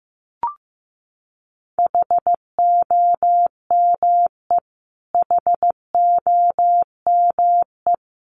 Morse code, heard as a keyed tone, about 14 wpm, 700 Hz. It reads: HOME HOME
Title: Top 200 Words Repeated 15wpm